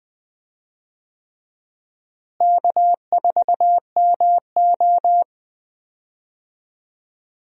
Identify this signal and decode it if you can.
Morse code, heard as a keyed tone, about 20 wpm, 700 Hz. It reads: K4MO
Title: USA Call Signs - 4 Characters - Repeated 3x - 20wpm